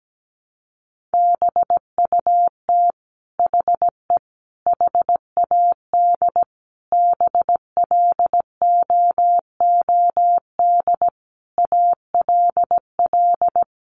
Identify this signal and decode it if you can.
Morse code, heard as a keyed tone, about 17 wpm, 700 Hz. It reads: BUT HE HAD BLOOD ALL
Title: Sentences from Top 1100 Words 17wpm